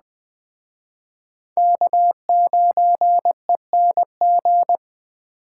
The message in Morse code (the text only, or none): K9ENG